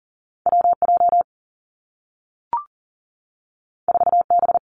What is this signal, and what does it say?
Morse code, heard as a keyed tone, about 40 wpm, 700 Hz. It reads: WJ 4B
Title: Two Challenging Letter-Number Combinations 40wpm